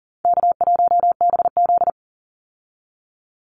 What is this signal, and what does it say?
Morse code, heard as a keyed tone, about 40 wpm, 700 Hz. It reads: K1BZ
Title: Easy Call Signs Repeated 2x - 40wpm